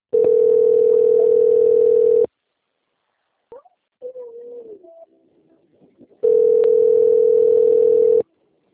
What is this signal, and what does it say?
Ringback tone (North America), 440 + 480 Hz.
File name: out- - - - - .73381